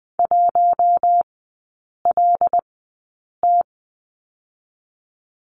Morse code, heard as a keyed tone, about 20 wpm, spaced as for 8 wpm, 700 Hz. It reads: 1LT
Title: 152 Lesson 25e - Letter p - Three Characters 20wpm